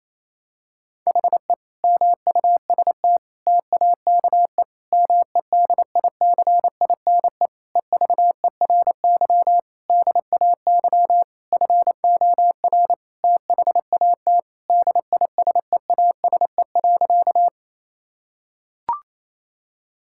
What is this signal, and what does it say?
Morse code, heard as a keyed tone, about 28 wpm, 700 Hz. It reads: HE MUST TAKE MEDICINE EVERY DAY FOR THAT DISEASE.
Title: Sentences from Top 1000 Words - Review Part 6 - 28wpm